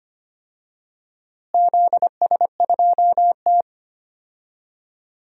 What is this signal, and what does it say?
Morse code, heard as a keyed tone, about 25 wpm, 700 Hz. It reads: ZS2T
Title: DXCC-100 Call Signs - 4 Characters - Repeated 2x - 25wpm